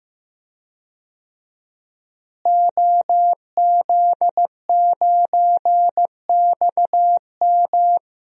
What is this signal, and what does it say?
Morse code, heard as a keyed tone, about 14 wpm, 700 Hz. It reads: OZ9XM
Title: DXCC-100 Call Signs - 5 Characters - Repeated 2x - 15wpm